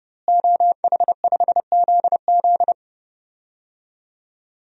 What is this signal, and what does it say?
Morse code, heard as a keyed tone, about 32 wpm, 700 Hz. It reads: OH5ZZ
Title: DXCC-100 Call Signs - 5 Characters - Repeated 3x - 30wpm